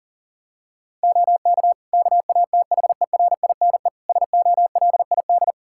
Morse code, heard as a keyed tone, about 40 wpm, 700 Hz. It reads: OK KATHERINE SOLID
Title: K1USN SST QSO Practice 40wpm